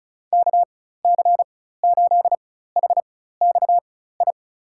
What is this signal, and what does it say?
Morse code, heard as a keyed tone, about 36 wpm, spaced as for 16 wpm, 700 Hz. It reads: KC8HXI
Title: USA Call Signs - 6 Characters - 4x Character Spacing - 35wpm